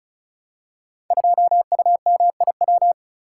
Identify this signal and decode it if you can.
Morse code, heard as a keyed tone, about 36 wpm, 700 Hz. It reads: 2UMIW
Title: Code Groups - 5 Letter:Numbers - 35wpm